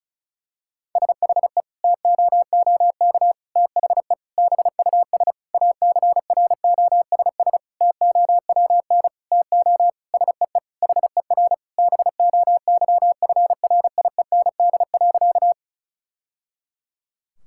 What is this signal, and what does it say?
Morse code, heard as a keyed tone, about 36 wpm, 700 Hz. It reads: SHE TOOK THE BUS ACROSS TOWN TO SEE HER BOYFRIEND.
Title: Sentences from Top 1000 Words - Review Part 6 - 35wpm